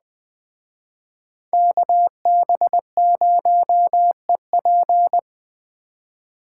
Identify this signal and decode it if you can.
Morse code, heard as a keyed tone, about 24 wpm, 700 Hz. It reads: KB0EP
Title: Speed Racing - USA 2x2 Call Signs 20wpm